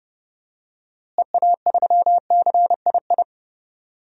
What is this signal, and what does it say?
Morse code, heard as a keyed tone, about 30 wpm, 700 Hz. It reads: EA3CII
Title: DXCC-100 Call Signs - 6 Characters - Repeated 2x - 30wpm